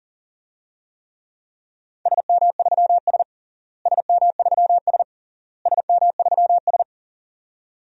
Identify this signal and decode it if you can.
Morse code, heard as a keyed tone, about 37 wpm, 700 Hz. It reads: SM3S SM3S SM3S
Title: DXCC-100 Call Signs - 4 Characters - Repeated 3x - 40wpm